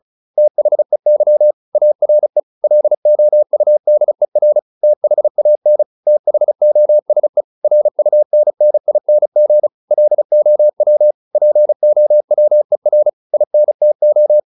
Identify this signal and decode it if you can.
Morse code, heard as a keyed tone, about 34 wpm, 600 Hz. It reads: THEY ARE LOUDER THAN THOSE RUNNING LOW POWER INTO